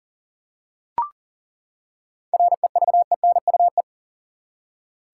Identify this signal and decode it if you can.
Morse code, heard as a keyed tone, about 40 wpm, 700 Hz. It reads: REVENUE
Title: Seven Letter Words - 40wpm